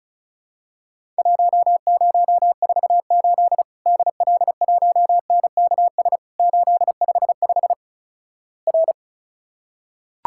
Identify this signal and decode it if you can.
Morse code, heard as a keyed tone, about 36 wpm, 700 Hz. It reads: 1048 DL1NKS 855 R
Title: Worked All Europe DX Contest - QTC Exchange - 35wpm